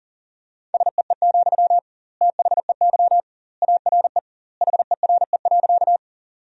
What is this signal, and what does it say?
Morse code, heard as a keyed tone, about 40 wpm, 700 Hz. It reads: SEE, THEY ARE HERE.
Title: 2x Word-Spacing - Sentences from Top 200 Words - 40wpm